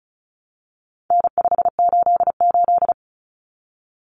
Morse code, heard as a keyed tone, about 34 wpm, 700 Hz. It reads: N588
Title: Common Cut Numbers - 4-Digit 35wpm